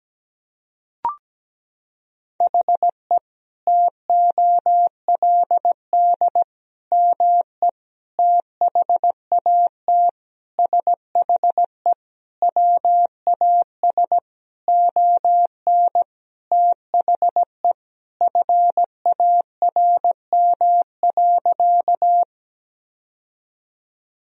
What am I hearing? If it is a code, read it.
Morse code, heard as a keyed tone, about 17 wpm, 700 Hz. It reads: HE TOLD ME THAT SHE WAS ON THE FARM.
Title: Sentences from Top 1000 Words - Review Part 2 - 17wpm